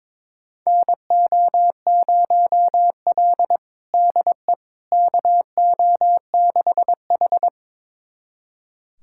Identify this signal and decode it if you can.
Morse code, heard as a keyed tone, about 22 wpm, 700 Hz. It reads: NO0L DE KO6H
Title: USA Callsign DE Callsign - 4 Characters - 22wpm